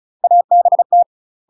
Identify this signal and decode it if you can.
Morse code, heard as a keyed tone, about 36 wpm, 700 Hz. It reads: ABT
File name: UK MASTER QSOs TRAINING / 35WPM / UK MASTER QSOs 0011